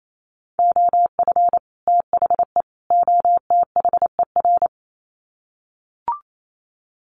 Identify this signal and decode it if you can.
Morse code, heard as a keyed tone, about 28 wpm, 700 Hz. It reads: OF THE OTHER
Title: Sets of 3 Words 28wpm